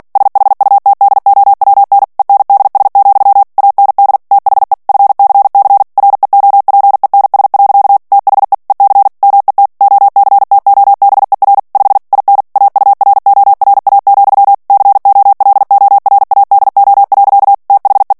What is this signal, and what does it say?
Morse code, heard as a keyed tone, about 48 wpm, 800 Hz. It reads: SHUTDOWN ENDS, AND THE FCC REOPENS. THE EC MET OCTOBER 5 IN AURORA, COLORADO. THE
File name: ANGLAIS texte clair 3 vitesse 45 mots minute